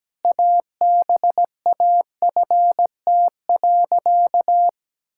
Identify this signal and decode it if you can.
Morse code, heard as a keyed tone, about 17 wpm, 700 Hz. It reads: ABAFT.
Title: Pangram Challenge v2 - 17wpm